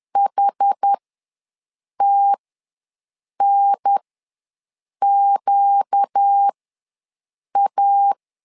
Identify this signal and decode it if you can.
Morse code, heard as a keyed tone, about 10 wpm, spaced as for 6 wpm, 800 Hz. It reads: HTNQA
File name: LEZIONI / 4 Q ws 10wpm 2306